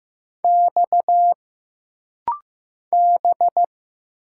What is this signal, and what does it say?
Morse code, heard as a keyed tone, about 15 wpm, 700 Hz. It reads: X B
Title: Single Letters - Rapid-Fire 15wpm